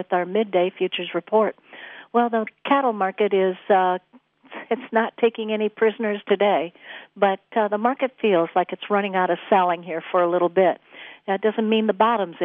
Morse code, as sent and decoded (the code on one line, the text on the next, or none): none